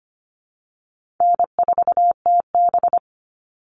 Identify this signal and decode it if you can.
Morse code, heard as a keyed tone, about 25 wpm, 700 Hz. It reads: N4TB
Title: Easy Call Signs Repeated 2x - 25wpm